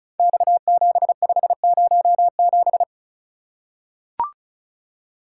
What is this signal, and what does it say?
Morse code, heard as a keyed tone, about 36 wpm, 700 Hz. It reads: X7507
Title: Code Groups - 5 Letter:Numbers - 35wpm